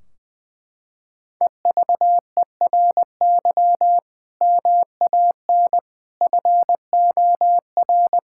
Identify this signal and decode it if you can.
Morse code, heard as a keyed tone, about 20 wpm, 700 Hz. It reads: EVERY MAN FOR
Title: Sentences from Top 1600 Words 20wpm